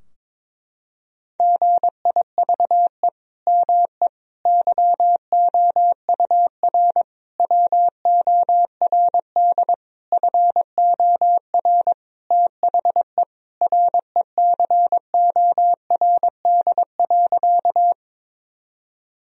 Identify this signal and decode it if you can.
Morse code, heard as a keyed tone, about 22 wpm, 700 Hz. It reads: GIVE ME YOUR WORD FOR THE RECORD.
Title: Sentences from Top 1000 Words - Review Part 3 - 22wpm